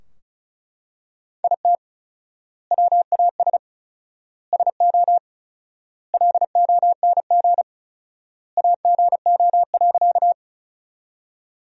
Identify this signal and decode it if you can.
Morse code, heard as a keyed tone, about 34 wpm, 700 Hz. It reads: IT WAS SO LONG AGO.
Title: 4x Word-Spacing - Sentences from Top 500 Words - 35wpm